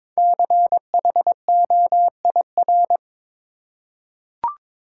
Morse code, heard as a keyed tone, about 22 wpm, 700 Hz. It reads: CHOIR
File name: 5 Letter Words - First 3 Letters Given - 22wpm